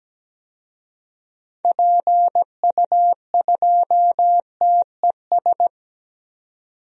Morse code, heard as a keyed tone, about 17 wpm, 700 Hz. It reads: PU2TES
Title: DXCC-100 Call Signs - 6 Characters - 17wpm